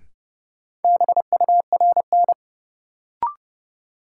Morse code, heard as a keyed tone, about 30 wpm, 700 Hz. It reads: BURN
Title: Sentences from Top 1300 Words - New Word Drill - 30wpm